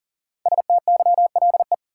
Morse code, heard as a keyed tone, about 39 wpm, 700 Hz. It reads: STYLE
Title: Five Letter Words - 40wpm